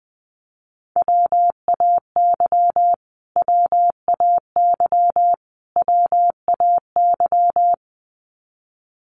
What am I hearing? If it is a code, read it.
Morse code, heard as a keyed tone, about 20 wpm, 700 Hz. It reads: WAY WAY WAY
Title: Top 200 Words Repeated 20wpm